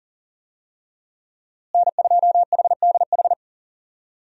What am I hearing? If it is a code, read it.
Morse code, heard as a keyed tone, about 40 wpm, 700 Hz. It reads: N2HDH
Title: USA Call Signs - 5 Characters - Repeated 2x - 40wpm